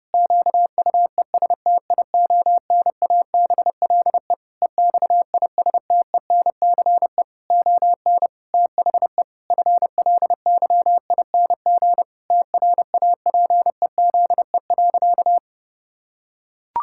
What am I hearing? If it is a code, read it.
Morse code, heard as a keyed tone, about 30 wpm, 700 Hz. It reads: QUESTIONABLE EXISTENCE ON THE FLYING TRAPEZE.
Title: Pangram Challenge v2 - 30wpm